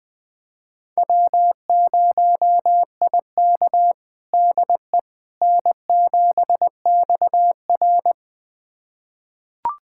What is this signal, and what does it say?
Morse code, heard as a keyed tone, about 20 wpm, 700 Hz. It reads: W0IK DE N7XR
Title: USA Callsign DE Callsign - 4 Characters - 20wpm